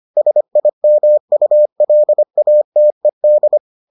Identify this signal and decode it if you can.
Morse code, heard as a keyed tone, about 25 wpm, 600 Hz. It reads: SIMULATED